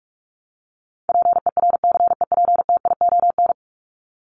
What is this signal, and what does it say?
Morse code, heard as a keyed tone, about 44 wpm, 700 Hz. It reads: PERCEPTION
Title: Top 1751-2000 Words 45wpm